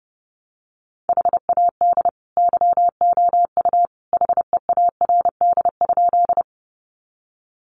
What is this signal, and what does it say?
Morse code, heard as a keyed tone, about 30 wpm, 700 Hz. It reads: HAD YOU HEARD?
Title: Sentences from Top 400 Words 30wpm